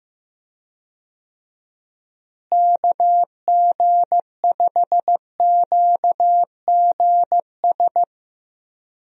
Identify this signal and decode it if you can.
Morse code, heard as a keyed tone, about 15 wpm, 700 Hz. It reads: KG5QGS
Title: USA Call Signs - 6 Characters - Repeated 3x - 15wpm